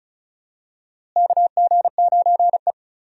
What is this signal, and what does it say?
Morse code, heard as a keyed tone, about 34 wpm, 700 Hz. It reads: KG9E
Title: USA Call Signs - 4 Characters - Repeated 2x - 35wpm